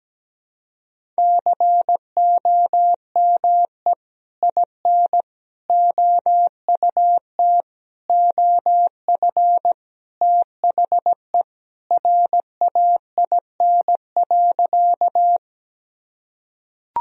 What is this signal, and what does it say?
Morse code, heard as a keyed tone, about 17 wpm, 700 Hz. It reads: COME IN OUT OF THE RAIN.
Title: Sentences from Top 1000 Words - Review Part 1 - 17wpm